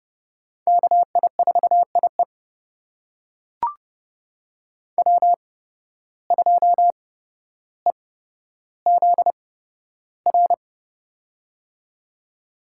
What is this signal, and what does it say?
Morse code, heard as a keyed tone, about 30 wpm, spaced as for 13 wpm, 700 Hz. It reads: KI4IE W2EZR
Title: USA Call Signs - 5 Characters - 8x Character Spacing - 30wpm